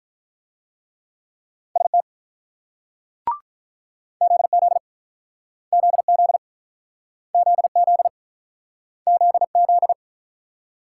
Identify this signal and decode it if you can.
Morse code, heard as a keyed tone, about 47 wpm, 700 Hz. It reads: ST ZZ ZZ ZZ ZZ